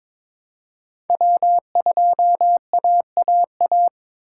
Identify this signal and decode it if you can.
Morse code, heard as a keyed tone, about 22 wpm, 700 Hz. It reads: W2AAA
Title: USA Call Signs - 5 Characters - 3x Character Spacing - 22wpm